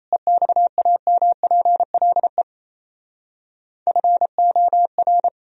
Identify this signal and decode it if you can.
Morse code, heard as a keyed tone, about 35 wpm, 700 Hz. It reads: EXAMPLE FOR
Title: Speed-Racing - Sets of 2 Words 28wpm